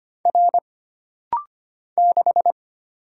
Morse code, heard as a keyed tone, about 25 wpm, 700 Hz. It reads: R 6
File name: Single Character (Letter:Number:Punctuation:Prosign) - Rapid-Fire - 25wpm